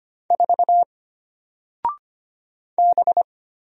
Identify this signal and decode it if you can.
Morse code, heard as a keyed tone, about 25 wpm, 700 Hz. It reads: 4 B
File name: Single-Letter-Number-25wpm